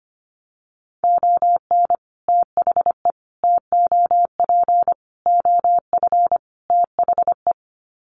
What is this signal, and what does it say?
Morse code, heard as a keyed tone, about 25 wpm, 700 Hz. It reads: ON THE TOP OF THE
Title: Sets of 5 Words Encore 25wpm